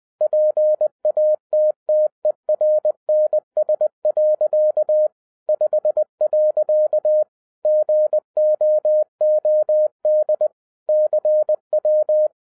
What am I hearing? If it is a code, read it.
Morse code, heard as a keyed tone, about 20 wpm, 610 Hz. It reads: PATTERNS. 5. GOOD CW
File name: 2011-10-mp3 / n13-TIPS FROM MY 34 YEARS OF EXPERIENCE 20wpm